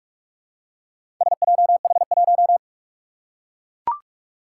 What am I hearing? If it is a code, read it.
Morse code, heard as a keyed tone, about 44 wpm, 700 Hz. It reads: SJH1
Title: Four Challenging Letter-Number Combinations 45wpm